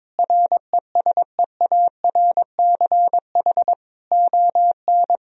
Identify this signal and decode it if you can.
Morse code, heard as a keyed tone, about 22 wpm, 700 Hz. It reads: RESEARCH ON
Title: Sentences from Top 1000 Words - Review Part 4 - 22wpm